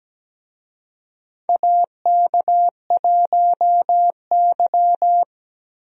Morse code, heard as a keyed tone, about 17 wpm, 700 Hz. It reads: AK1Y
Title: USA Call Signs - 4 Characters - Repeated 3x - 17wpm